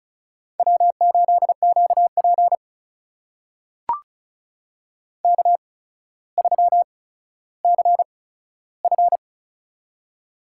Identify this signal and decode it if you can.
Morse code, heard as a keyed tone, about 36 wpm, spaced as for 15 wpm, 700 Hz. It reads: W8QP K3CF
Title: USA Call Signs - 4 Characters - 8x Character Spacing - 35wpm